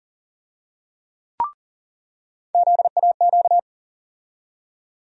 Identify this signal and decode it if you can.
Morse code, heard as a keyed tone, about 40 wpm, 700 Hz. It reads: ZAQ
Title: USA Call Sign Postfixes - 3 Letters 40wpm